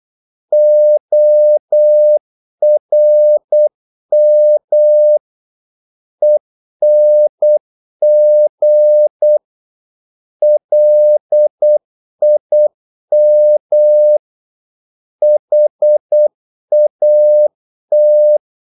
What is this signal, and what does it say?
Morse code, heard as a keyed tone, about 8 wpm, 600 Hz. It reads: ORM ENG LIM HAT
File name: Korte ord DK 8wpm